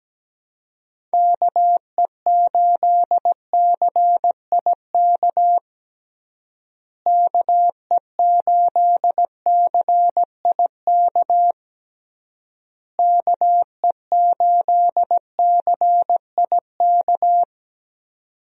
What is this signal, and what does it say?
Morse code, heard as a keyed tone, about 17 wpm, 700 Hz. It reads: KE8CIK KE8CIK KE8CIK
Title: USA Call Signs - 6 Characters - Repeated 3x - 17wpm